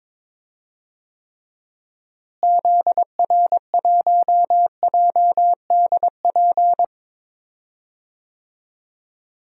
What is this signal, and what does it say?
Morse code, heard as a keyed tone, about 22 wpm, 700 Hz. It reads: ZR1JDP
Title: DXCC-100 Call Signs - 6 Characters - Repeated 2x - 22wpm